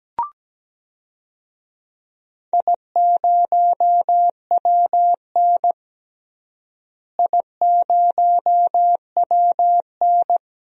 Morse code, heard as a keyed tone, about 17 wpm, 700 Hz. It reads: I0WN I0WN
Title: DXCC-100 Call Signs - 4 Characters - Repeated 3x - 17wpm